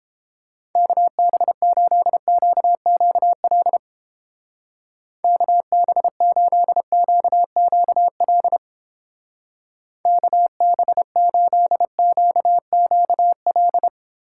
Morse code, heard as a keyed tone, about 29 wpm, 700 Hz. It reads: KB8QQL KB8QQL KB8QQL